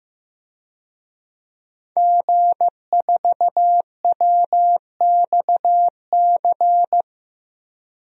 Morse code, heard as a keyed tone, about 15 wpm, 700 Hz. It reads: G4WXC
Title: DXCC-100 Call Signs - 5 Characters - Repeated 2x - 15wpm